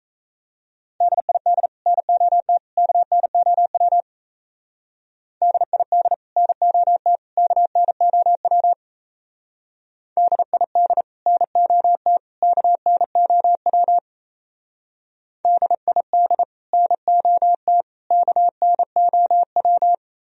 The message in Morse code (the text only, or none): DID NOT KNOW DID NOT KNOW DID NOT KNOW DID NOT KNOW